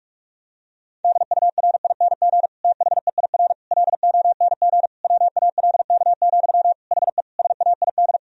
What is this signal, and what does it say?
Morse code, heard as a keyed tone, about 46 wpm, 700 Hz. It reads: DURING THEIR LONG WALK, HE SAID
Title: Sentences from Top 1000 Words - Review Part 2 - 45wpm